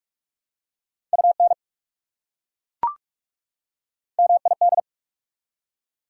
Morse code, heard as a keyed tone, about 42 wpm, 690 Hz. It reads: UN MID
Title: Prefixes and Suffixes 45wpm